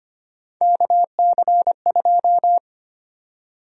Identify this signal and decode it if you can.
Morse code, heard as a keyed tone, about 25 wpm, 700 Hz. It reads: KC2
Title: USA Call Sign Prefixes - 25wpm